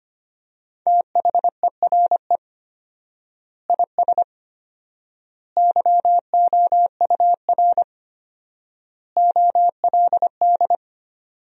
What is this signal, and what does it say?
Morse code, heard as a keyed tone, about 25 wpm, 700 Hz. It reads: THERE IS YOUR OLD
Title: 4x Word-Spacing - Sentences from Top 500 Words - 25wpm